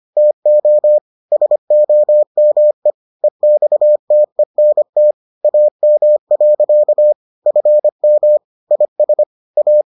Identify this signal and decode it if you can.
Morse code, heard as a keyed tone, about 25 wpm, 600 Hz. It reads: TO SOME EXTENT AM. FM IS A